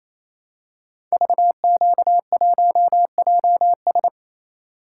Morse code, heard as a keyed tone, about 27 wpm, 700 Hz. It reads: VQ1JS
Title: Code Groups - 5 Letter:Numbers - 28wpm